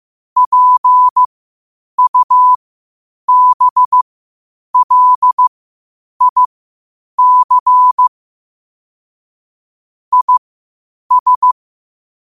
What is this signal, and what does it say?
Morse code, heard as a keyed tone, about 15 wpm, spaced as for 8 wpm, 1000 Hz. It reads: PUBLIC IS
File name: qotdfiles / 8wpm